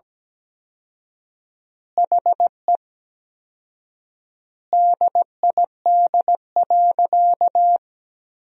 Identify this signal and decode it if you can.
Morse code, heard as a keyed tone, about 17 wpm, 700 Hz. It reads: HE DID.
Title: 4x Word-Spacing - Sentences from Top 500 Words - 17wpm